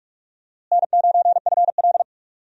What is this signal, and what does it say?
Morse code, heard as a keyed tone, about 42 wpm, 700 Hz. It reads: N9FL
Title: USA Call Signs - 4 Characters - Repeated 3x - 45wpm